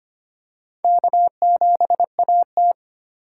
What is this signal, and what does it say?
Morse code, heard as a keyed tone, about 25 wpm, 700 Hz. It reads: K7AT
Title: USA Call Signs - 4 Characters - 2x Character Spacing - 25wpm